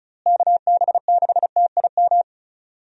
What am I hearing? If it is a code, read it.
Morse code, heard as a keyed tone, about 36 wpm, 700 Hz. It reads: KB6TIM
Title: USA Call Signs - 6 Characters - Repeated 3x - 35wpm